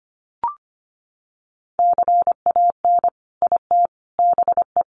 Morse code, heard as a keyed tone, about 25 wpm, 700 Hz. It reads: CAN IT BE
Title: Sentences from Top 500 Words 25wpm